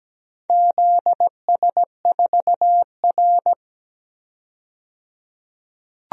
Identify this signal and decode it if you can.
Morse code, heard as a keyed tone, about 17 wpm, 700 Hz. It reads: ZS4R
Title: DXCC-100 Call Signs - 4 Characters - 17wpm